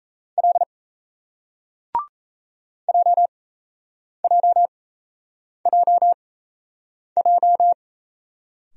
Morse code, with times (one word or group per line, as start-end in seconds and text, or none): 0.4-0.6 L
2.9-7.7 JJJJ